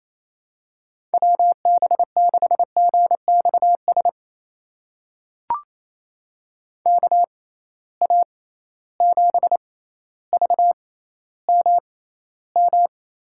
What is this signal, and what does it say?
Morse code, heard as a keyed tone, about 28 wpm, spaced as for 14 wpm, 700 Hz. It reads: WB6GXS KA7VMM
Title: USA Call Signs - 6 Characters - 6x Character Spacing - 28wpm